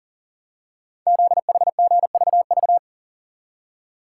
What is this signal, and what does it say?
Morse code, heard as a keyed tone, about 42 wpm, 690 Hz. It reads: ZHGVV
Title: Code Groups - 5 Letters - 40wpm